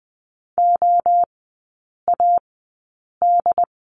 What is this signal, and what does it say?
Morse code, heard as a keyed tone, about 20 wpm, spaced as for 8 wpm, 700 Hz. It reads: OAD
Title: 067 Lesson 12e - Letter d - Three Characters 20wpm